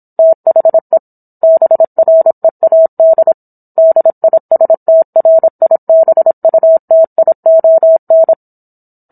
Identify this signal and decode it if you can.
Morse code, heard as a keyed tone, about 27 wpm, 650 Hz. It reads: THE BREAD DISTRIBUTION